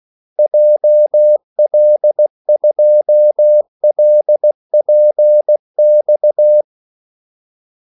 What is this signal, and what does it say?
Morse code, heard as a keyed tone, about 16 wpm, 600 Hz. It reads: JL2LPX
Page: Kaldesignaler 16 wpm